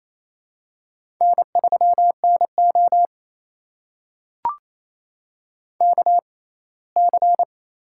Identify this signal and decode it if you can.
Morse code, heard as a keyed tone, about 27 wpm, spaced as for 19 wpm, 700 Hz. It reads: N3NO KC
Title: USA Call Signs - 4 Characters - 6x Character Spacing - 28wpm